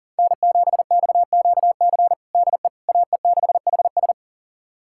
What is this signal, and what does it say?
Morse code, heard as a keyed tone, about 40 wpm, 700 Hz. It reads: N7XQC DE AE6HS
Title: USA Callsign DE Callsign - 5 Characters - 40wpm